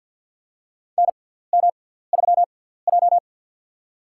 Morse code, heard as a keyed tone, about 51 wpm, spaced as for 17 wpm, 700 Hz. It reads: NM3J
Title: USA Call Signs - 6x Character Spacing - 50wpm